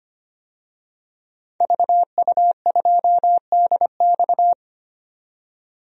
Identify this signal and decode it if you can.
Morse code, heard as a keyed tone, about 25 wpm, 700 Hz. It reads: VU2DX
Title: DXCC-100 Call Signs - 5 Characters - 25wpm